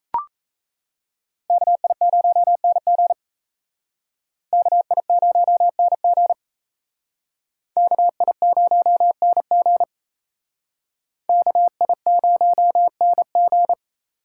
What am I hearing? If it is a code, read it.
Morse code, heard as a keyed tone, about 36 wpm, 700 Hz. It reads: KI0NG KI0NG KI0NG KI0NG